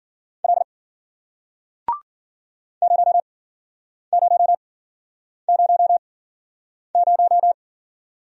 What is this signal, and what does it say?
Morse code, heard as a keyed tone, about 53 wpm, spaced as for 9 wpm, 700 Hz. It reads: L 0000